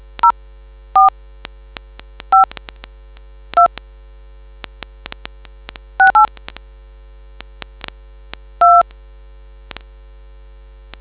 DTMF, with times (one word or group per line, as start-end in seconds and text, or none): 0.2-1.1 04
2.3-2.4 5
3.6-3.7 2
6.0-6.3 68
8.6-8.8 2